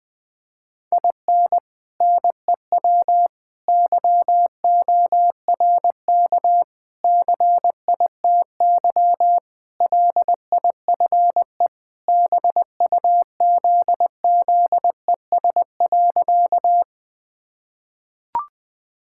Morse code, heard as a keyed tone, about 20 wpm, 700 Hz. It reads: IN NEW YORK CITY LIFE BUZZES.
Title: Sentences with 4-Word Phrases - 20wpm